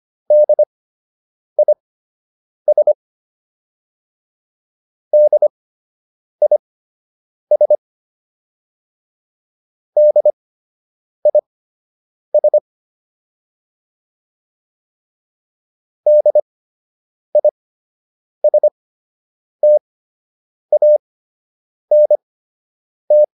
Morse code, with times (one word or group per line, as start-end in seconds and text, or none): 0.3-2.9 DIS
5.1-7.8 DIS
10.0-12.6 DIS
16.1-23.2 DISTANT